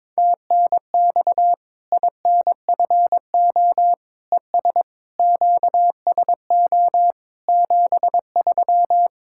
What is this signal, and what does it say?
Morse code, heard as a keyed tone, about 22 wpm, 700 Hz. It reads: TNX INFO ES QSO 73
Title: K1USN SST QSO Practice 22wpm